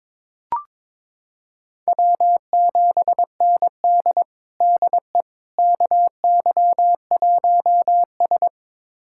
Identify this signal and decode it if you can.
Morse code, heard as a keyed tone, about 22 wpm, 700 Hz. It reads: W7ND DE KY1S
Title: USA Callsign DE Callsign - 4 Characters - 22wpm